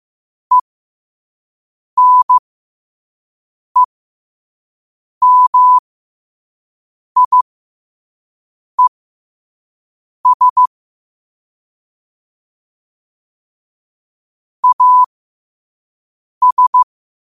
Morse code, heard as a keyed tone, about 15 wpm, spaced as for 5 wpm, 1000 Hz. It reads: ENEMIES AS
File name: qotdfiles / 5wpm